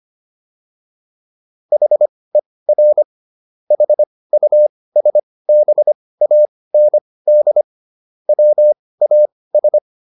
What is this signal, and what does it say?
Morse code, heard as a keyed tone, about 25 wpm, spaced as for 18 wpm, 600 Hz. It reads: HER HUSBAND WAS